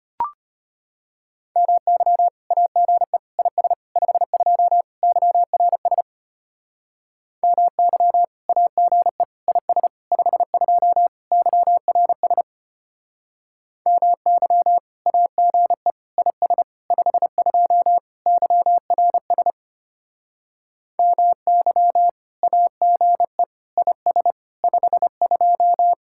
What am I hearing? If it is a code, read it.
Morse code, heard as a keyed tone, about 33 wpm, 700 Hz. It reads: MY AGE IS 52 YRS MY AGE IS 52 YRS MY AGE IS 52 YRS MY AGE IS 52